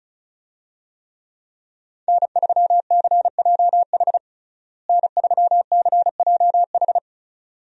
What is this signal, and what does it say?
Morse code, heard as a keyed tone, about 34 wpm, 700 Hz. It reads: N3CJH N3CJH
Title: USA Call Signs - 5 Characters - Repeated 2x - 35wpm